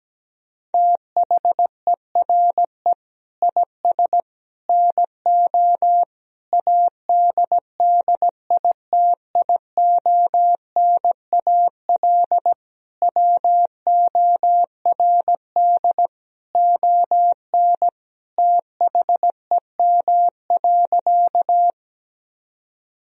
Morse code, heard as a keyed tone, about 17 wpm, 700 Hz. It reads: THERE IS NO ADDITIONAL WORD ON THEM.
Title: Sentences from Top 1000 Words - Review Part 4 - 17wpm